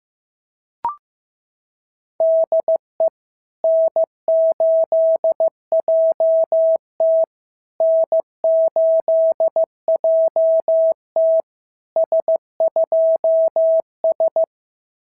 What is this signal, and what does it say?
Morse code, heard as a keyed tone, about 14 wpm, 650 Hz. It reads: DE N8JT N8JT S2S
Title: SOTA QSO Practice 15wpm